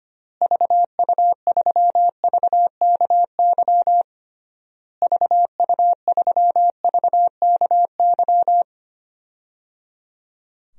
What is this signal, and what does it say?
Morse code, heard as a keyed tone, about 25 wpm, 700 Hz. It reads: VU3VKY VU3VKY
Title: DXCC-100 Call Signs - 6 Characters - Repeated 3x - 25wpm